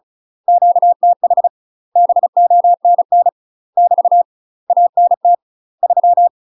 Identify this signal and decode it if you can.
Morse code, heard as a keyed tone, about 35 wpm, 700 Hz. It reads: QTH BONN <BT> ANT 3
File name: UK MASTER QSOs TRAINING / 35WPM / UK MASTER QSOs 0032